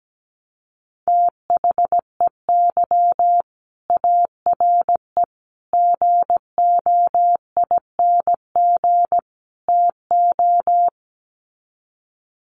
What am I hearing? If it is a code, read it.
Morse code, heard as a keyed tone, about 17 wpm, 700 Hz. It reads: THEY ARE GOING TO
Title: Sets of 4 Words 17wpm